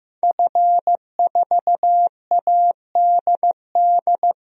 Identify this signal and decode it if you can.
Morse code, heard as a keyed tone, about 15 wpm, 700 Hz. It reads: F4ADD
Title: DXCC-100 Call Signs - 5 Characters - Repeated 2x - 15wpm